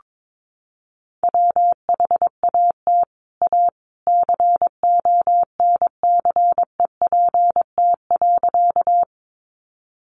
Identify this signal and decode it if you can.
Morse code, heard as a keyed tone, about 22 wpm, 700 Hz. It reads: WHAT A CONCEPT.
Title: Sentences from Top 1100 Words 22wpm